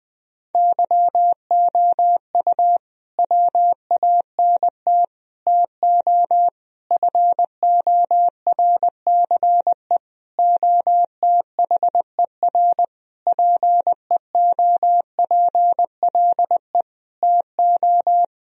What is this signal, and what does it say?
Morse code, heard as a keyed tone, about 20 wpm, 700 Hz. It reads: YOU WANT TO FORCE OTHER PEOPLE TO
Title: Sentences from Top 1000 Words - Review Part 3 - 20wpm